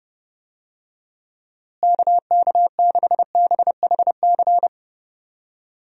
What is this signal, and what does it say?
Morse code, heard as a keyed tone, about 32 wpm, 700 Hz. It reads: KK6BHC
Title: USA Call Signs - 6 Characters - Repeated 2x - 30wpm